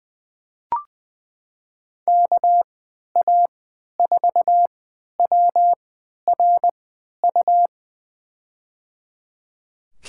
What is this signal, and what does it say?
Morse code, heard as a keyed tone, about 20 wpm, spaced as for 11 wpm, 700 Hz. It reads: KA4WRU
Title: USA Call Signs - 6 Characters - 3x Character Spacing - 20wpm